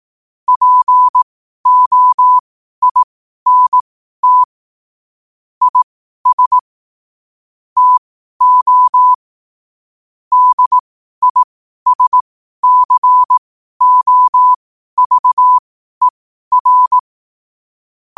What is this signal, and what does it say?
Morse code, heard as a keyed tone, about 18 wpm, spaced as for 12 wpm, 1000 Hz. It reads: POINT IS TO DISCOVER
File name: qotdfiles / 12wpm